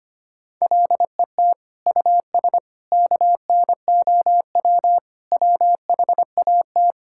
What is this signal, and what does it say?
Morse code, heard as a keyed tone, about 25 wpm, 700 Hz. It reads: LET US KNOW WHAT
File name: Sets of 9 Words - 25wpm